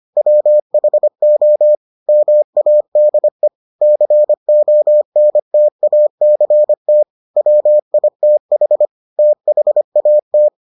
Morse code, heard as a keyed tone, about 25 wpm, 600 Hz. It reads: WHO MADE CONTACT WITH THAT